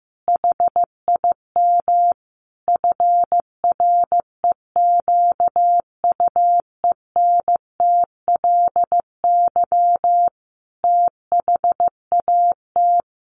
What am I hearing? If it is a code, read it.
Morse code, heard as a keyed tone, about 15 wpm, 700 Hz. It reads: HIM FREQUENTLY THAT
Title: Sentences from Top 1300 Words 15wpm